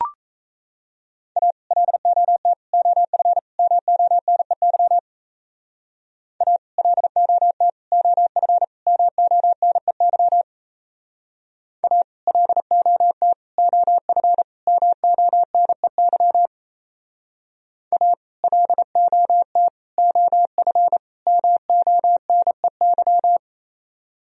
Morse code, heard as a keyed tone, about 34 wpm, 700 Hz. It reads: A LOT OF MONEY A LOT OF MONEY A LOT OF MONEY A LOT OF MONEY
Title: Speed-Racing - Sets of 4 Words 28wpm